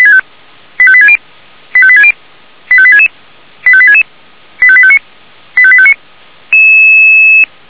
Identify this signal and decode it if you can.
Five-tone selective call (ZVEI), heard as quick runs of five tones, tones of about 70 ms. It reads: 85860 85860 85866 85866 85859 85859
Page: SDR/ZVEI